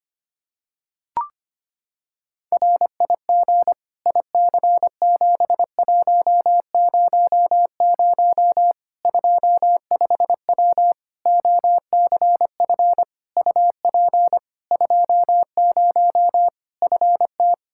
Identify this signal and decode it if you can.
Morse code, heard as a keyed tone, about 25 wpm, 700 Hz. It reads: RIG IC7100 25W OCF UP 20 FT
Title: K1USN SST QSO Practice 25wpm